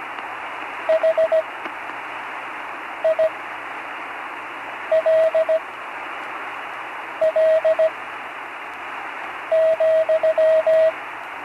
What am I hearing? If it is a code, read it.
Morse code, heard as a keyed tone, about 17 wpm, spaced as for 5 wpm, 640 Hz, with radio noise. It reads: HILL,
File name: CW Learning Programs / Learning Morse Code (from Tune in the World) / 43 on the air